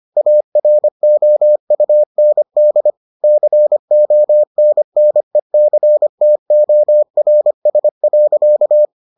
Morse code, heard as a keyed tone, about 25 wpm, 600 Hz. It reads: AROUND CONNECTORS.